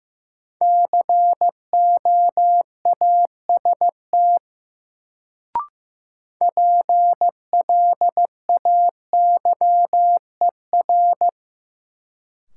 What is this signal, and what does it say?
Morse code, heard as a keyed tone, about 15 wpm, 700 Hz. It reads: COAST PLAYER
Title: Sentences from Top 1000 Words - Review Part 4 - New Word Drill - 15wpm